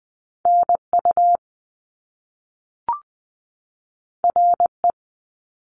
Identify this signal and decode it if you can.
Morse code, heard as a keyed tone, about 20 wpm, 700 Hz. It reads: N U R E
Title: Top Two Letter Combinations in Words 20wpm